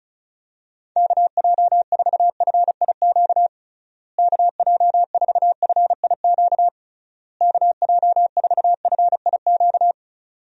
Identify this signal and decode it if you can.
Morse code, heard as a keyed tone, about 34 wpm, 700 Hz. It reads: KJ4FIQ KJ4FIQ KJ4FIQ
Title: USA Call Signs - 6 Characters - Repeated 3x - 35wpm